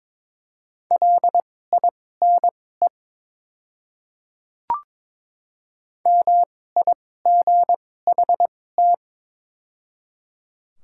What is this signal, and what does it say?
Morse code, heard as a keyed tone, about 22 wpm, spaced as for 16 wpm, 700 Hz. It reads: LINE MIGHT
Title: 2x Character Spacing - Top 500 Words - 22wpm